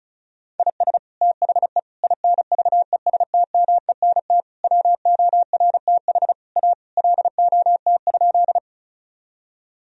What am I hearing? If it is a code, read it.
Morse code, heard as a keyed tone, about 36 wpm, 700 Hz. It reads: IS THE INVESTMENT WORTH A LOT?
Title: Sentences from Top 1000 Words - Review Part 4 - 35wpm